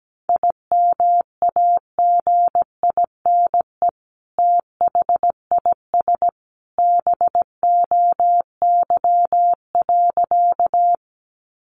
Morse code, heard as a keyed tone, about 17 wpm, 700 Hz. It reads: IMAGINE THIS BOY.
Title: Sentences from Top 1400 Words 17wpm